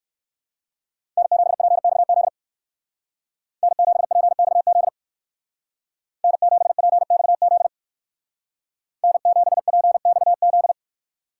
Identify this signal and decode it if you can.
Morse code, heard as a keyed tone, about 59 wpm, 700 Hz. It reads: N7PXZ N7PXZ N7PXZ N7PXZ